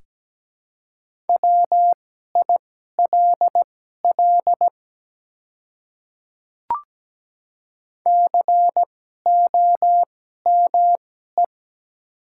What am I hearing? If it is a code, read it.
Morse code, heard as a keyed tone, about 17 wpm, spaced as for 12 wpm, 700 Hz. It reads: WILL COME
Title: 2x Character Spacing - Top 100 Words - 17wpm